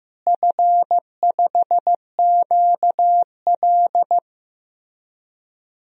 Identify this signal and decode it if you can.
Morse code, heard as a keyed tone, about 15 wpm, 700 Hz. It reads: F5QL
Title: DXCC-100 Call Signs - 4 Characters - Repeated 3x - 15wpm